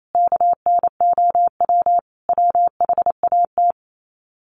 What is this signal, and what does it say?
Morse code, heard as a keyed tone, about 28 wpm, 700 Hz. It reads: KNOW WHAT
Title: Sets of 5 Words 28wpm